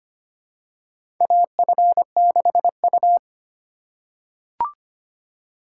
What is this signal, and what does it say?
Morse code, heard as a keyed tone, about 25 wpm, 700 Hz. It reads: AF6U
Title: USA Call Signs - 4 Characters - 3x Character Spacing - 25wpm